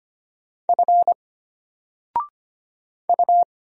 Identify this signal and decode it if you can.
Morse code, heard as a keyed tone, about 25 wpm, 700 Hz. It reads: F U
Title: Single Letter-Number 25wpm